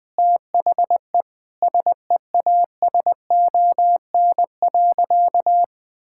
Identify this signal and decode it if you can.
Morse code, heard as a keyed tone, about 20 wpm, 700 Hz. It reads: THE SEASON.
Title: Sentences with 6-Word Phrases - 20wpm